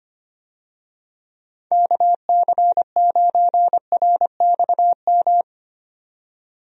Morse code, heard as a keyed tone, about 25 wpm, 700 Hz. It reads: KC9RXM
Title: USA Call Signs - 6 Characters - 25wpm